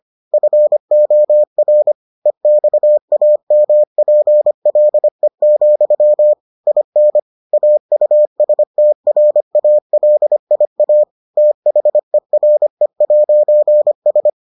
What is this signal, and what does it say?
Morse code, heard as a keyed tone, about 25 wpm, 600 Hz. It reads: FOR EXAMPLE, IN AUSTRALIA THERE'S